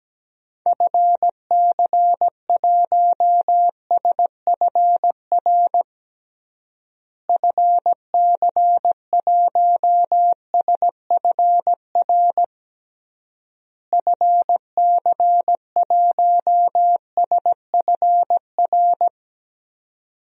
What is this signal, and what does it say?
Morse code, heard as a keyed tone, about 17 wpm, 700 Hz. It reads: FC1SFR FC1SFR FC1SFR
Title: DXCC-100 Call Signs - 6 Characters - Repeated 3x - 17wpm